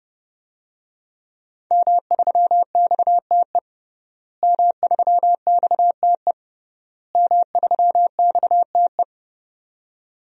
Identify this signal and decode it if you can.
Morse code, heard as a keyed tone, about 32 wpm, 700 Hz. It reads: M3XTE M3XTE M3XTE
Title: DXCC-100 Call Signs - 5 Characters - Repeated 3x - 30wpm